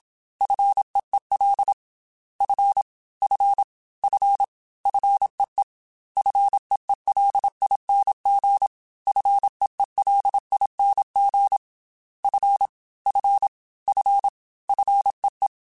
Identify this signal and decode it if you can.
Morse code, heard as a keyed tone, about 27 wpm, 800 Hz. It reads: FEEL F F F FEE FEELING FEELING F F F FEE
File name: F and L / F and L mp3 / F and L Prefixes 3x2 Twice @25